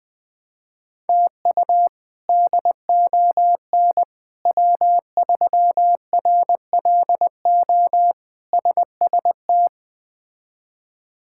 Morse code, heard as a keyed tone, about 20 wpm, 700 Hz. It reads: TU DON W3RLO SST
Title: K1USN SST QSO Practice - 20wpm